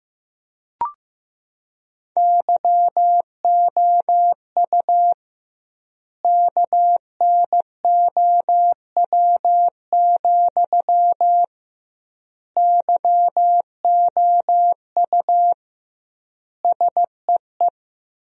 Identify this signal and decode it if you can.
Morse code, heard as a keyed tone, about 15 wpm, 700 Hz. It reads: YOU KNOW, YOU SEE
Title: 2x Word-Spacing - Sentences from Top 200 Words - 15wpm